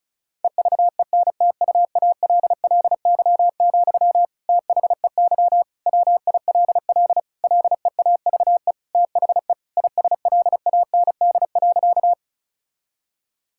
Morse code, heard as a keyed tone, about 34 wpm, 700 Hz. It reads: EVENTUALLY, THEY WILL LEAVE THE ISLAND.
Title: Sentences from Top 1000 Words - Review Part 6 - 35wpm